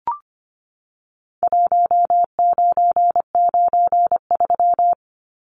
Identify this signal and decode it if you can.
Morse code, heard as a keyed tone, about 25 wpm, 700 Hz. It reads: 1993
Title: Numbers - Contest Style - 4-Digit 25wpm